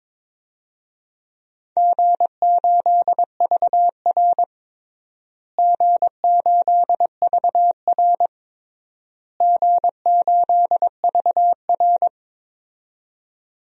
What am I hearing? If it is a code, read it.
Morse code, heard as a keyed tone, about 22 wpm, 700 Hz. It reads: G8VR G8VR G8VR
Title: DXCC-100 Call Signs - 4 Characters - Repeated 3x - 22wpm